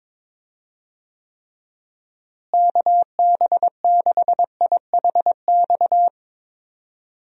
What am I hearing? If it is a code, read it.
Morse code, heard as a keyed tone, about 22 wpm, 700 Hz. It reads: KB6IHX
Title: USA Call Signs - 6 Characters - Repeated 3x - 22wpm